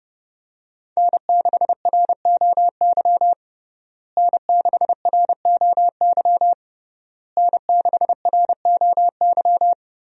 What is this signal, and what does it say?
Morse code, heard as a keyed tone, about 30 wpm, 700 Hz. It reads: N6ROY N6ROY N6ROY
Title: USA Call Signs - 5 Characters - Repeated 3x - 30wpm